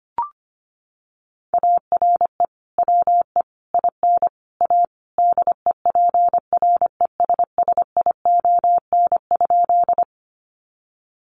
Morse code, heard as a keyed tone, about 25 wpm, 700 Hz. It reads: ARE WE IN A DEPRESSION?
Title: Sentences from Top 1300 Words 25wpm